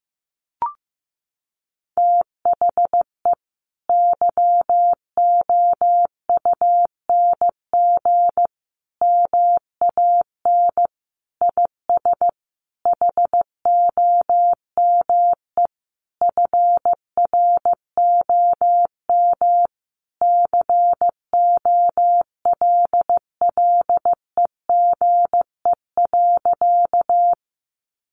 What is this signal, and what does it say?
Morse code, heard as a keyed tone, about 15 wpm, 700 Hz. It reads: THE YOUNG MAN IS HOME FROM COLLEGE.
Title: Sentences from Top 1300 Words 15wpm